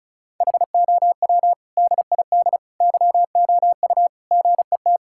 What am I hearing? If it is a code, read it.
Morse code, heard as a keyed tone, about 34 wpm, 700 Hz. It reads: HOW DID YOU GET
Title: Sentences from Top 1000 Words - Review Part 5 - 35wpm